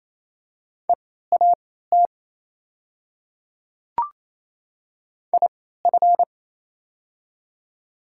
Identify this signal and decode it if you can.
Morse code, heard as a keyed tone, about 27 wpm, spaced as for 16 wpm, 700 Hz. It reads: EAT IF
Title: 3x Character Spacing - Top 300 Words - 28wpm